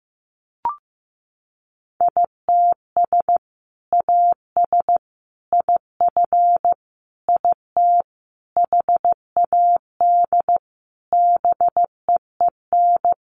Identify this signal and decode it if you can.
Morse code, heard as a keyed tone, about 15 wpm, 700 Hz. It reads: ITS AS IF IT HAD BEEN
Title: Sentences from Top 800 Words 15wpm